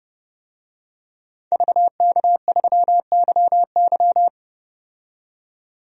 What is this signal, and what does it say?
Morse code, heard as a keyed tone, about 30 wpm, 700 Hz. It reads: VK3YY
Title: DXCC-100 Call Signs - 5 Characters - 30wpm